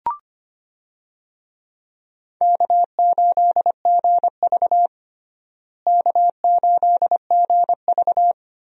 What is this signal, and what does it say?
Morse code, heard as a keyed tone, about 25 wpm, 700 Hz. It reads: K8GV K8GV
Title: USA Call Signs - 4 Characters - Repeated 2x - 25wpm